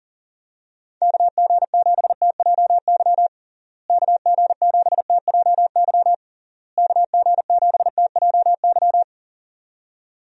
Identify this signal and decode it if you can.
Morse code, heard as a keyed tone, about 40 wpm, 700 Hz. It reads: KG7TJY KG7TJY KG7TJY
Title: USA Call Signs - 6 Characters - Repeated 3x - 40wpm